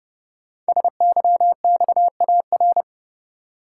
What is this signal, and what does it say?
Morse code, heard as a keyed tone, about 30 wpm, 700 Hz. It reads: SYXAR
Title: Code Groups - 5 Letters - 30wpm